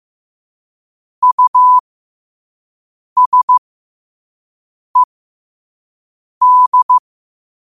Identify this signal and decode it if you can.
Morse code, heard as a keyed tone, about 15 wpm, spaced as for 5 wpm, 1000 Hz. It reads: USED